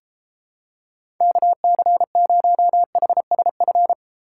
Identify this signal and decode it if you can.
Morse code, heard as a keyed tone, about 31 wpm, 700 Hz. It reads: KC0HSF